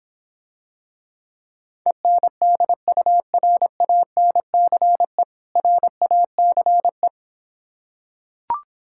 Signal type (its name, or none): Morse code